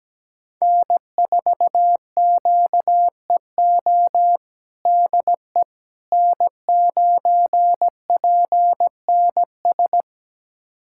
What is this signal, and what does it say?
Morse code, heard as a keyed tone, about 17 wpm, 700 Hz. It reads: N4QEO DE N9PNS
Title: USA Callsign DE Callsign - 5 Characters - 17wpm